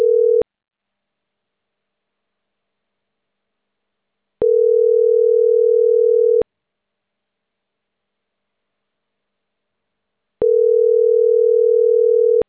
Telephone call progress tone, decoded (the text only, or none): ringback tone